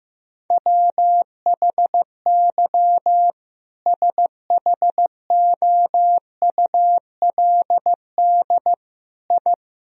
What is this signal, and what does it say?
Morse code, heard as a keyed tone, about 15 wpm, 700 Hz. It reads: WHY SHOULD I
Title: Sentences from Top 1000 Words - Review Part 3 - 15wpm